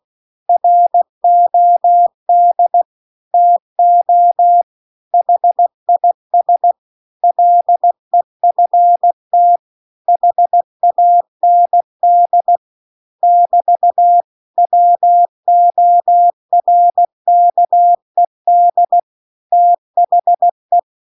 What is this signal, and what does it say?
Morse code, heard as a keyed tone, about 16 wpm, 700 Hz. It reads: ROD TO HIS LEFT HAND <BT> WORKED THE